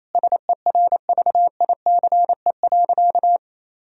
Morse code, heard as a keyed tone, about 28 wpm, 700 Hz. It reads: SERVICE.
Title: Sentences from Top 1000 Words - Review Part 4 - 28wpm